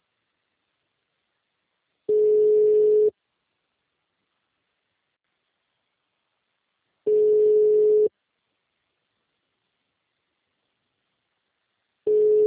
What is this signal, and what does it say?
Ringback tone (Europe, 425 Hz).
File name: out- -2010- - - .22